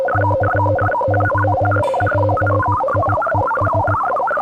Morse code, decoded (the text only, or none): COPIES